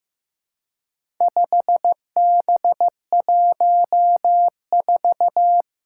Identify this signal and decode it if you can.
Morse code, heard as a keyed tone, about 15 wpm, 700 Hz. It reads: 5B14
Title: Four Challenging Letter-Number Combinations 15wpm